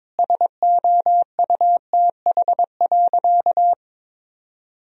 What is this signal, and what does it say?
Morse code, heard as a keyed tone, about 22 wpm, 700 Hz. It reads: SOUTH.
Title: Sentences from Top 1000 Words - Review Part 2 - 22wpm